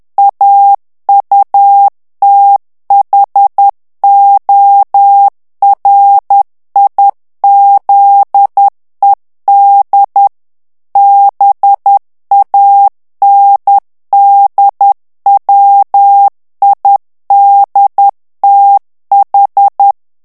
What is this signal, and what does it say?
Morse code, heard as a keyed tone, about 10 wpm, 800 Hz. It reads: AUTHORIZED BANDWIDTH
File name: ANGLAIS texte clair 3 vitesse 10 mots minute